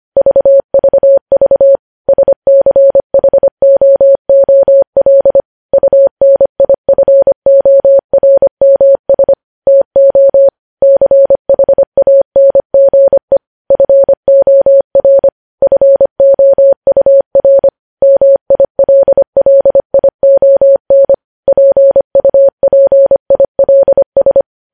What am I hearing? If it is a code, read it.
Morse code, heard as a keyed tone, about 25 wpm, 550 Hz. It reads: VVV SCHOOL UNIFORMS TO CHANGE FOR FOUR MILLION PUPILS
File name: podcast / News headlines-1x-25x25